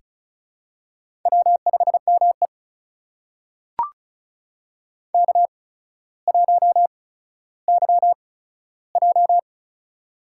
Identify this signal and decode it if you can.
Morse code, heard as a keyed tone, about 36 wpm, spaced as for 15 wpm, 700 Hz. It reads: W5ME K1YJ
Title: USA Call Signs - 4 Characters - 8x Character Spacing - 35wpm